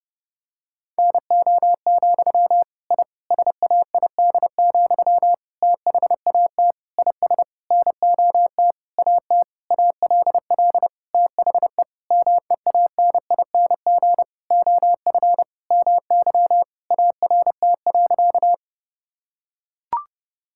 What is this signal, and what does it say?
Morse code, heard as a keyed tone, about 30 wpm, 700 Hz. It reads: NO, I SAID, THAT IS NOT AT ALL THE MEANING OF MY ART.
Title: Sentences from Top 1000 Words - Review Part 6 - 30wpm